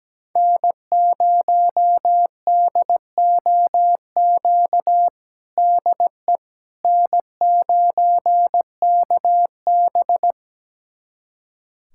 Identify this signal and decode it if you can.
Morse code, heard as a keyed tone, about 17 wpm, 700 Hz. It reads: N0DOQ DE N9KB
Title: USA Callsign DE Callsign 17wpm